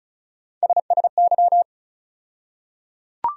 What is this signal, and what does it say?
Morse code, heard as a keyed tone, about 34 wpm, 700 Hz. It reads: SSY
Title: USA Call Sign Postfixes - 3 Letters 35wpm